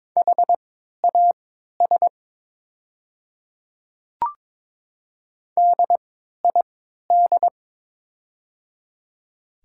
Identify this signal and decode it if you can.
Morse code, heard as a keyed tone, about 23 wpm, spaced as for 13 wpm, 700 Hz. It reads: HAS DID
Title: 3x Character Spacing - Top 100 Words - 22wpm